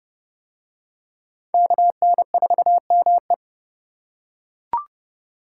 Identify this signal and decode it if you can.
Morse code, heard as a keyed tone, about 27 wpm, 700 Hz. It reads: KN4ME